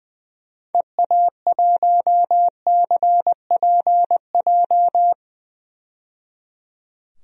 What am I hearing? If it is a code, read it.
Morse code, heard as a keyed tone, about 20 wpm, 700 Hz. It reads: EA1CPJ
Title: DXCC-100 Call Signs - 6 Characters - 20wpm